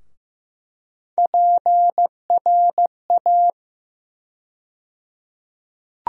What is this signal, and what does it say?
Morse code, heard as a keyed tone, about 15 wpm, 700 Hz. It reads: PRA
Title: USA Call Sign Postfixes - 3 Letters 15wpm